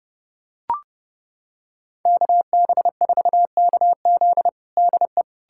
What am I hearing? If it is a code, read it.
Morse code, heard as a keyed tone, about 29 wpm, 700 Hz. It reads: KB4KZ DE
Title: USA Callsign DE Callsign 30wpm